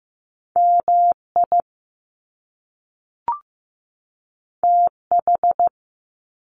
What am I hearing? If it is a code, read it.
Morse code, heard as a keyed tone, about 15 wpm, 700 Hz. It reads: MI TH
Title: Top Two Letter Combinations in Words 15wpm